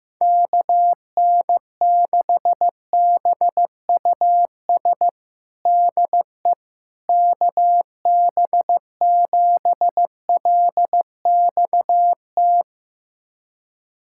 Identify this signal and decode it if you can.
Morse code, heard as a keyed tone, about 15 wpm, 700 Hz. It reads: KN6BUS DE KB7LXT
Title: USA Callsign DE Callsign - 6 Characters - 15wpm